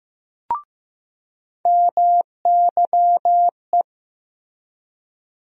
Morse code, heard as a keyed tone, about 16 wpm, 700 Hz. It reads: MYE
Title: USA Call Sign Postfixes - 3 Letters 15wpm